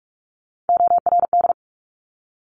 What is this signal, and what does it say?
Morse code, heard as a keyed tone, about 46 wpm, 700 Hz. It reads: ORD
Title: Top Three Letter Combinations in Words 45wpm